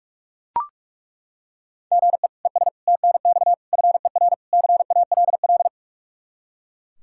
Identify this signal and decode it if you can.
Morse code, heard as a keyed tone, about 42 wpm, 700 Hz. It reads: GE ES TNX FER CALL
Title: K1USN SST QSO Practice 45wpm